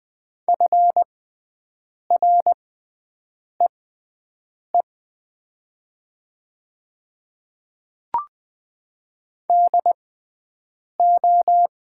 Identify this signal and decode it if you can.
Morse code, heard as a keyed tone, about 20 wpm, spaced as for 7 wpm, 700 Hz. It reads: FREE DO
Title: 6x Character Spacing - Top 500 Words - 20wpm